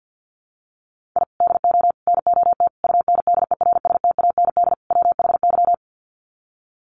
Morse code, heard as a keyed tone, about 50 wpm, 700 Hz. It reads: I DO NOT UNDERSTAND WHY
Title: Sets of 5 Words Encore 50wpm